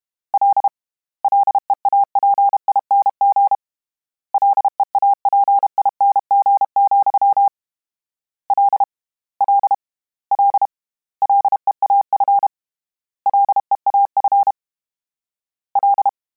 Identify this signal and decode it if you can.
Morse code, heard as a keyed tone, about 32 wpm, 800 Hz. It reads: L LEAPING LEAPING, L L L LEAF LEAF L
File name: Combinations / Combinations-mp3s / Combinations@30